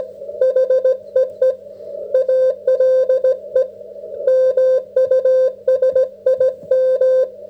Morse code, heard as a keyed tone, about 17 wpm, 530 Hz. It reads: HEE ALE MUSIM